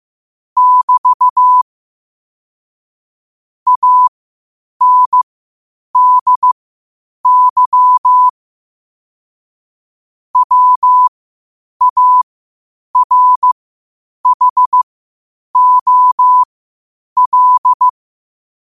Morse code, heard as a keyed tone, about 15 wpm, spaced as for 8 wpm, 1000 Hz. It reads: <BT> ANDY WARHOL
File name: qotdfiles / 8wpm